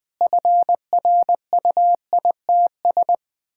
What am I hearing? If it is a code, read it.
Morse code, heard as a keyed tone, about 21 wpm, 700 Hz. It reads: FRUITS
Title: Sentences with 6-Word Phrases - 20wpm